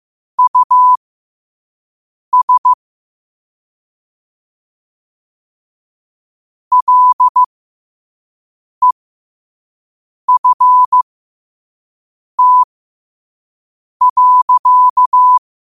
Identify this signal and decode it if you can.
Morse code, heard as a keyed tone, about 15 wpm, spaced as for 5 wpm, 1000 Hz. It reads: US LEFT.
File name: qotdfiles / 5wpm